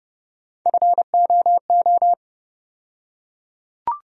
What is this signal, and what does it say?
Morse code, heard as a keyed tone, about 32 wpm, 700 Hz. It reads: FOO
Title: USA Call Sign Postfixes - 3 Letters 30wpm